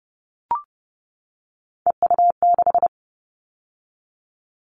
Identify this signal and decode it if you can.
Morse code, heard as a keyed tone, about 30 wpm, 700 Hz. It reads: EU6
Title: Callsign Prefixes for Top 100 DXCC - 30wpm